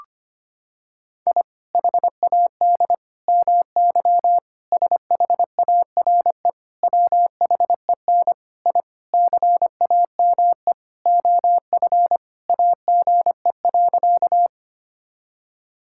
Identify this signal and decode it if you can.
Morse code, heard as a keyed tone, about 25 wpm, 700 Hz. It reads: I HAD MY SHARE WHEN I CAME OF AGE.
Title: Sentences from Top 1000 Words - Review Part 6 - 25wpm